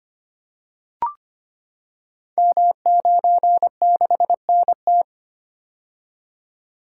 Morse code, heard as a keyed tone, about 25 wpm, 700 Hz. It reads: M96NT
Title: Code Groups - 5 Letter:Numbers - 25wpm